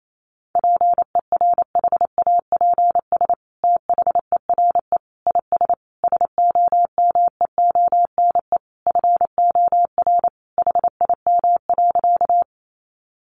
Morse code, heard as a keyed tone, about 28 wpm, 700 Hz. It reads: PERHAPS THERE IS SOMEONE FOR HIM.
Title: Sentences from Top 600 Words 28wpm